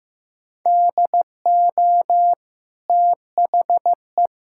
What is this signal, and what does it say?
Morse code, heard as a keyed tone, about 15 wpm, 700 Hz. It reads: DO THE
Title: Sentences from Top 1000 Words - Review Part 6 - 15wpm